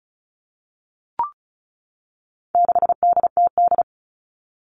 Morse code, heard as a keyed tone, about 34 wpm, 700 Hz. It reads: 6DTD
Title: All Cut Numbers - 4-Digit 35wpm